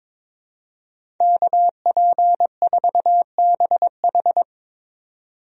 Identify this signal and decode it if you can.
Morse code, heard as a keyed tone, about 22 wpm, 700 Hz. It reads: KP4BH
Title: DXCC-100 Call Signs - 5 Characters - 22wpm